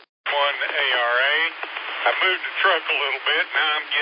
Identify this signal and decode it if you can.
Emergency Alert System (SAME) tones, heard as harsh buzzing bursts, with speech. It is heard from 1.1 to 3.7 s.